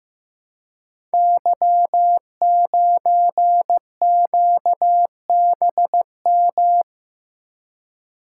Morse code, heard as a keyed tone, about 15 wpm, 700 Hz. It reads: Y9QBM
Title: Code Groups - 5 Letter:Numbers - 15wpm